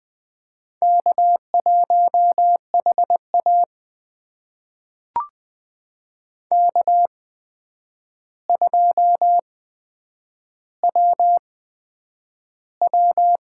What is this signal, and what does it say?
Morse code, heard as a keyed tone, about 20 wpm, spaced as for 9 wpm, 700 Hz. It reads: K1HA K2WW
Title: USA Call Signs - 4 Characters - 8x Character Spacing - 20wpm